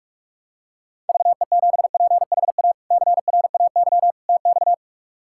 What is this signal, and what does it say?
Morse code, heard as a keyed tone, about 47 wpm, 700 Hz. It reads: VE7PHA CRAY TX
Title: K1USN SST QSO Practice - 45wpm